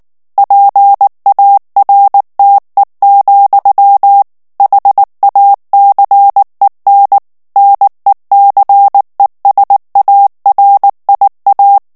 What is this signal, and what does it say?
Morse code, heard as a keyed tone, about 19 wpm, 800 Hz. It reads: PARTE, HACEN NECESARIA
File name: morse mp3 / ESPAGNOL TEXTE EN CLAIR 2 VITESSE 18 MOTS MINUTES